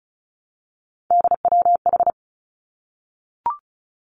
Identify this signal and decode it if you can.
Morse code, heard as a keyed tone, about 36 wpm, 700 Hz. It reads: DWH
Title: Three Challenging Letter-Number Combinations 35wpm